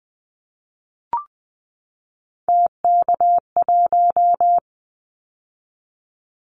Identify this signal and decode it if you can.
Morse code, heard as a keyed tone, about 20 wpm, 700 Hz. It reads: TK1
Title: Callsign Prefixes for Top 100 DXCC - 20wpm